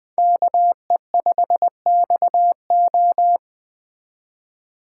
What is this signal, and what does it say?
Morse code, heard as a keyed tone, about 20 wpm, 700 Hz. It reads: KE5XO